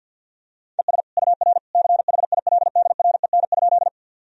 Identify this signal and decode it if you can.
Morse code, heard as a keyed tone, about 47 wpm, 700 Hz. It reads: ES UR CHILDREN?
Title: Sentences from Top 300 Words and Abbreviations 50wpm